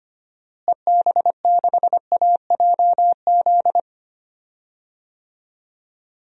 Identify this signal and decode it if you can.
Morse code, heard as a keyed tone, about 25 wpm, 700 Hz. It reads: EB6AJZ
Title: DXCC-100 Call Signs - 6 Characters - Repeated 3x - 25wpm